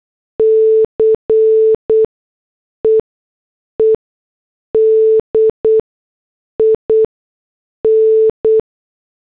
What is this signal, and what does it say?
Morse code, heard as a keyed tone, about 8 wpm, spaced as for 6 wpm, 440 Hz. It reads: CEEDIN